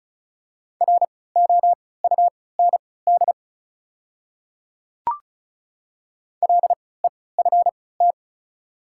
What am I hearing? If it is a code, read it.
Morse code, heard as a keyed tone, about 36 wpm, spaced as for 20 wpm, 700 Hz. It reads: ROUND LEFT
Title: 3x Character Spacing - Top 300 Words - 35wpm